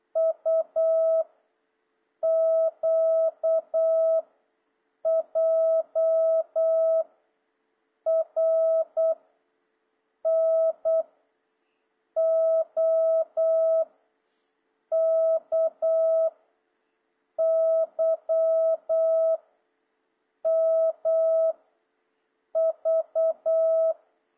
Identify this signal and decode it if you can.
Morse code, heard as a keyed tone, about 8 wpm, 650 Hz. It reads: U Q J R N O K Y M V
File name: jh8cbh / 2026cw